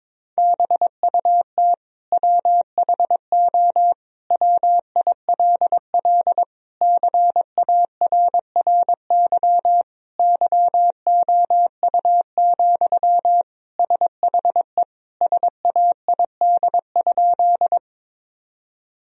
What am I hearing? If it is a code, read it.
Morse code, heard as a keyed tone, about 22 wpm, 700 Hz. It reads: BUT WHO WILL CARRY YOU, SHE SAID?
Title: Sentences from Top 1000 Words - Review Part 1 - 22wpm